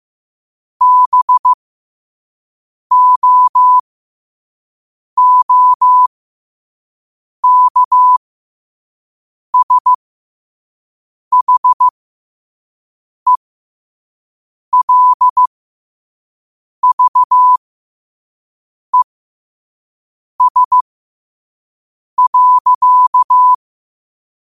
Morse code, heard as a keyed tone, about 15 wpm, spaced as for 5 wpm, 1000 Hz. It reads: BOOKSHELVES.
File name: qotdfiles / 5wpm